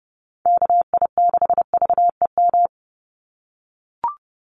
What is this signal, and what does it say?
Morse code, heard as a keyed tone, about 30 wpm, 700 Hz. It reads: KI6VEM
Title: USA Call Signs 30wpm